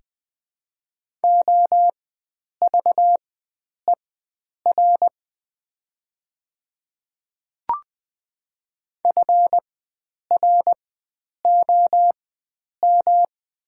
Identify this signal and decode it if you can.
Morse code, heard as a keyed tone, about 20 wpm, spaced as for 9 wpm, 700 Hz. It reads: OVER FROM
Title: 4x Character Spacing - Top 100 Words - 20wpm